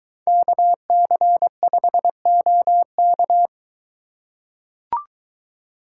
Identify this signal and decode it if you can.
Morse code, heard as a keyed tone, about 23 wpm, 700 Hz. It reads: KC5OK